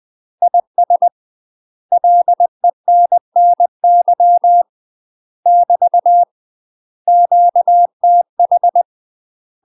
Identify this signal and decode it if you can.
Morse code, heard as a keyed tone, about 20 wpm, 700 Hz. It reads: IS LENNY <BT> QTH
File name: UK MASTER QSOs TRAINING / 20WPM / UK MASTER QSOs 0067